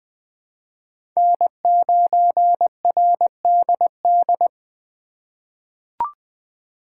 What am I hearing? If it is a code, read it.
Morse code, heard as a keyed tone, about 20 wpm, 700 Hz. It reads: N9RDD
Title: USA Call Signs - 5 Characters - 4x Character Spacing - 20wpm